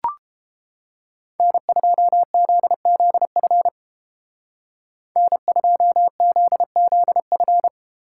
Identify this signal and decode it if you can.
Morse code, heard as a keyed tone, about 32 wpm, 700 Hz. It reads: N2ZZF N2ZZF